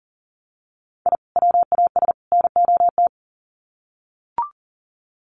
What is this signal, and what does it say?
Morse code, heard as a keyed tone, about 40 wpm, 700 Hz. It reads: I WAS NOT
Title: Sets of 3 Words 40wpm